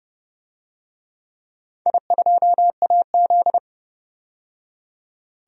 Morse code, heard as a keyed tone, about 32 wpm, 700 Hz. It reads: I2AZ
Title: DXCC-100 Call Signs - 4 Characters - 30wpm